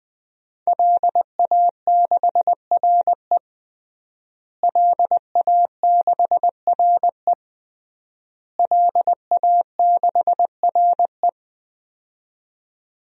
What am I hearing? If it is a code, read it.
Morse code, heard as a keyed tone, about 20 wpm, 700 Hz. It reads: LA6RE LA6RE LA6RE
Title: DXCC-100 Call Signs - 5 Characters - Repeated 3x - 20wpm